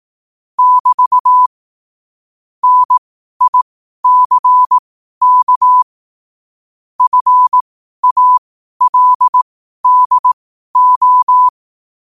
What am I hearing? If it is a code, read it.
Morse code, heard as a keyed tone, about 18 wpm, spaced as for 12 wpm, 1000 Hz. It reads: <BT> NICK FALDO
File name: qotdfiles / 12wpm